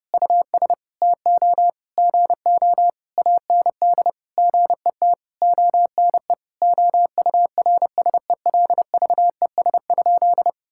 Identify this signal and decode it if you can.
Morse code, heard as a keyed tone, about 32 wpm, 700 Hz. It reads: US TO GO AND GET ONE OURSELVES?
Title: Sentences from Top 1600 Words 30wpm